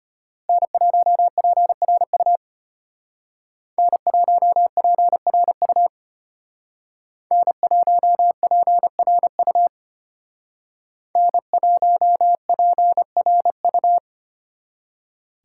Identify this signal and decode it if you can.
Morse code, heard as a keyed tone, about 30 wpm, 700 Hz. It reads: N1PRU N1PRU N1PRU N1PRU
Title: Speed Racing - USA 1x3 Call Signs 25wpm